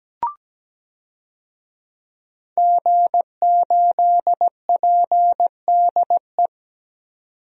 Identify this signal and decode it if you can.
Morse code, heard as a keyed tone, about 17 wpm, 700 Hz. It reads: G8PDE
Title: DXCC-100 Call Signs - 5 Characters - 17wpm